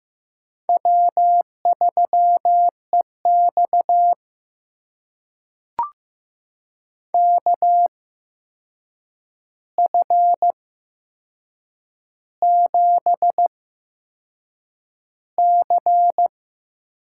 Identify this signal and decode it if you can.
Morse code, heard as a keyed tone, about 15 wpm, spaced as for 6 wpm, 700 Hz. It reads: W3EX KF7C
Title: USA Call Signs - 8x Character Spacing - 15wpm